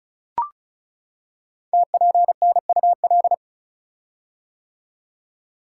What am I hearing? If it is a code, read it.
Morse code, heard as a keyed tone, about 36 wpm, 690 Hz. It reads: TPNUL
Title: Code Groups - 5 Letters - 35wpm